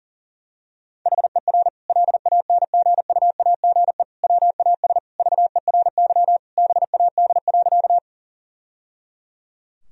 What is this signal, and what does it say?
Morse code, heard as a keyed tone, about 37 wpm, 700 Hz. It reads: HER LANGUAGE WAS VERY BAD.
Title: Sentences from Top 1000 Words - Review Part 3 - 40wpm